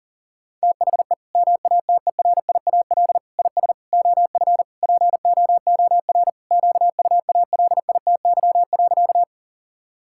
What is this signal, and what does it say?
Morse code, heard as a keyed tone, about 37 wpm, 700 Hz. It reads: THE MATERIAL IS OF POOR QUALITY.
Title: Sentences from Top 1000 Words - Review Part 5 - 40wpm